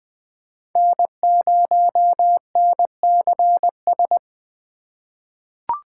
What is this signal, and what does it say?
Morse code, heard as a keyed tone, about 21 wpm, 700 Hz. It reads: N0NCS
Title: USA Call Signs - 5 Characters - 3x Character Spacing - 20wpm